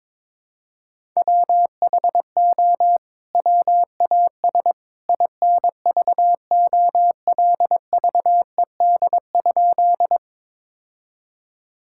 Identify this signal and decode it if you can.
Morse code, heard as a keyed tone, about 22 wpm, 700 Hz. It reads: WHO WAS INVOLVED?
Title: Sentences from Top 1000 Words - Review Part 5 - 22wpm